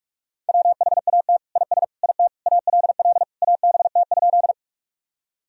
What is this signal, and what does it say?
Morse code, heard as a keyed tone, about 46 wpm, 700 Hz. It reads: WHAT IS IT ALL ABT?
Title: Sentences from Top 100 Words and Abbreviations 45wpm